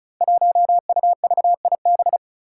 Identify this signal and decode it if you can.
Morse code, heard as a keyed tone, about 34 wpm, 700 Hz. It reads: 1UVIB
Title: Five Challenging Letter-Number Combinations 35wpm